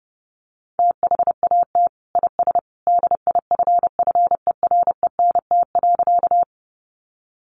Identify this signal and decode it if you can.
Morse code, heard as a keyed tone, about 30 wpm, 700 Hz. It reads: THAT IS DIFFERENT.
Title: Sentences from Top 600 Words 30wpm